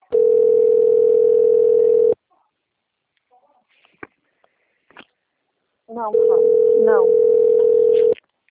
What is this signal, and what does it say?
Ringback tone (North America), 440 + 480 Hz, with speech.
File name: rg-1600-unknown- - - .68880